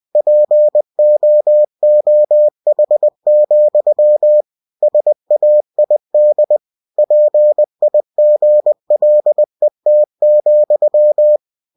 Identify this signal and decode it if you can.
Morse code, heard as a keyed tone, about 20 wpm, 600 Hz. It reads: POOH, SAID PIGLET,
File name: cw-books / winnie-the-pooh / winnie-20wpm-page-0034